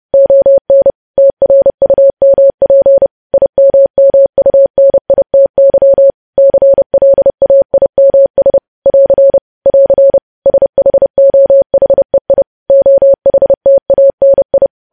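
Morse code, heard as a keyed tone, about 30 wpm, 550 Hz. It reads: ON TRUMP IMMUNITY CLAIMS <AR> <AR> SHOHEI OHTANI
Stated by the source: Morse news headlines sent at 30 wpm.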